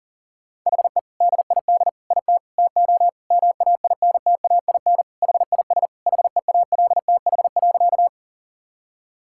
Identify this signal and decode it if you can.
Morse code, heard as a keyed tone, about 40 wpm, 700 Hz. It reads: HE DID IT TO MAINTAIN HIS HEALTH.
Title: Sentences from Top 1000 Words - Review Part 5 - 40wpm